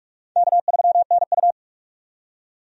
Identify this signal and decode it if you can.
Morse code, heard as a keyed tone, about 41 wpm, 700 Hz. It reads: K3NU
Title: USA Call Signs - 4 Characters - 45wpm